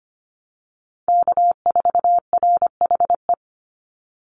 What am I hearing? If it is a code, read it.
Morse code, heard as a keyed tone, about 25 wpm, 700 Hz. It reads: K4RHE
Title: USA Call Signs 25wpm